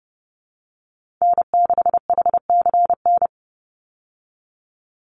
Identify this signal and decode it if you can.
Morse code, heard as a keyed tone, about 30 wpm, 700 Hz. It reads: N6HCN
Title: Call Signs Repeated 2x - 30wpm